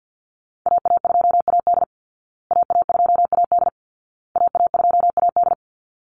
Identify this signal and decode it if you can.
Morse code, heard as a keyed tone, about 50 wpm, 700 Hz. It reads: AA2AD AA2AD AA2AD
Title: Call Signs Repeated 2x - 50wpm